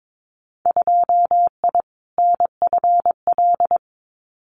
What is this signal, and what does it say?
Morse code, heard as a keyed tone, about 22 wpm, 700 Hz. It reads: 2I NFL
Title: Winter Field Day Exchanges 22wpm